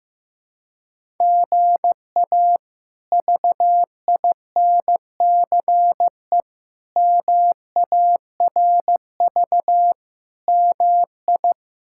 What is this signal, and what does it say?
Morse code, heard as a keyed tone, about 15 wpm, 700 Hz. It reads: GA VINCE MARV MI
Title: K1USN SST QSO Practice - 15wpm